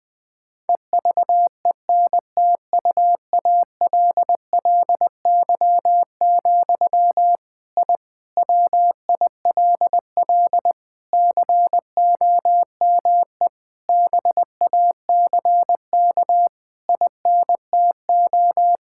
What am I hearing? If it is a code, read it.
Morse code, heard as a keyed tone, about 20 wpm, 700 Hz. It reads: EVENTUALLY, I WILL COME BACK INTO
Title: Sentences from Top 1000 Words - Review Part 6 - 20wpm